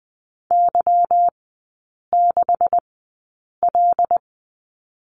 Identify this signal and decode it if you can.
Morse code, heard as a keyed tone, about 20 wpm, spaced as for 8 wpm, 700 Hz. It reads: Y6L
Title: 191 Lesson 31e - Letter v - Three Characters 20wpm